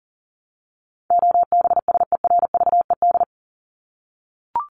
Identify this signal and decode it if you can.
Morse code, heard as a keyed tone, about 40 wpm, 700 Hz. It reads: OBSERVED
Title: ED Suffix - Top 100 Words with Base Words 40wpm